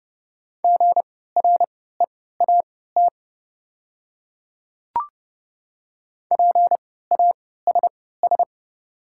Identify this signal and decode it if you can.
Morse code, heard as a keyed tone, about 30 wpm, spaced as for 17 wpm, 690 Hz. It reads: GREAT PASS
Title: 3x Character Spacing - Top 500 Words - 30wpm